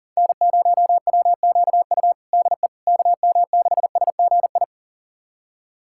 Morse code, heard as a keyed tone, about 40 wpm, 700 Hz. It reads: N0WQU DE KM6SGI
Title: USA Callsign DE Callsign 40wpm